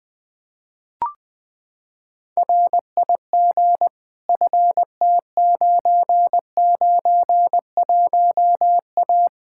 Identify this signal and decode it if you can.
Morse code, heard as a keyed tone, about 20 wpm, 700 Hz. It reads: RIG FT991A
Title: K1USN SST QSO Practice 20wpm